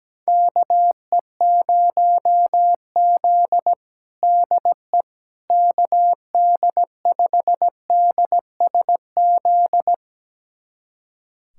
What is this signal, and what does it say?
Morse code, heard as a keyed tone, about 17 wpm, 700 Hz. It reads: KE0Z DE KD5DSZ
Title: USA Callsign DE Callsign 17wpm